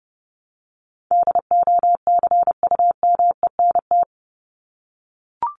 Morse code, heard as a keyed tone, about 30 wpm, 700 Hz. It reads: DOCUMENT
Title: Top 1751-2000 Words 30wpm